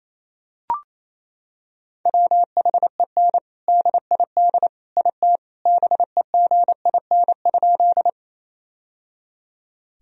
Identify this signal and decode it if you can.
Morse code, heard as a keyed tone, about 28 wpm, 700 Hz. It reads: WHEN DID IT BEGIN?
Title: Sentences from Top 1000 Words - Review Part 1 - 28wpm